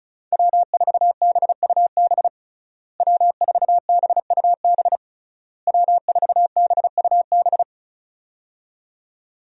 Morse code, heard as a keyed tone, about 36 wpm, 700 Hz. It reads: W4BUB W4BUB W4BUB
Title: USA Call Signs - 5 Characters - Repeated 3x - 35wpm